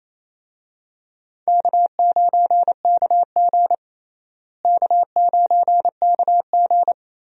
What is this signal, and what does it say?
Morse code, heard as a keyed tone, about 28 wpm, 700 Hz. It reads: K9KG K9KG
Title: USA Call Signs - 4 Characters - Repeated 3x - 28wpm